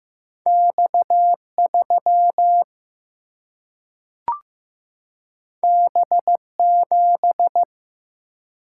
Morse code, heard as a keyed tone, about 15 wpm, 700 Hz. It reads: X3 B7
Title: Single-Letter Single-Number 15wpm